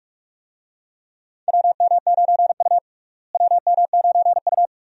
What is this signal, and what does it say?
Morse code, heard as a keyed tone, about 46 wpm, 700 Hz. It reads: WM9U WM9U
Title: USA Call Signs - 4 Characters - Repeated 2x - 45wpm